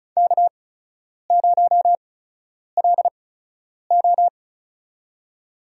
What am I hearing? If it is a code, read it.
Morse code, heard as a keyed tone, about 34 wpm, spaced as for 10 wpm, 700 Hz. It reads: K0LO
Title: USA Call Signs - 4 Characters - 8x Character Spacing - 35wpm